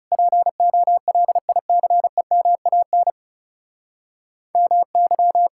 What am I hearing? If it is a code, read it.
Morse code, heard as a keyed tone, about 37 wpm, 700 Hz. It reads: POLICEMAN MY
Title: Speed-Racing Common QSO Phrases - 30wpm